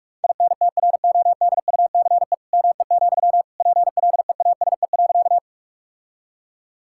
Morse code, heard as a keyed tone, about 46 wpm, 700 Hz. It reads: INTRODUCE ME, PLEASE.
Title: Sentences from Top 1600 Words 45wpm